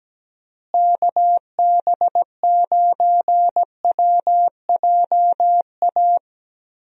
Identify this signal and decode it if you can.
Morse code, heard as a keyed tone, about 17 wpm, 700 Hz. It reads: KB9WJA
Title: USA Call Signs - 6 Characters - Repeated 3x - 17wpm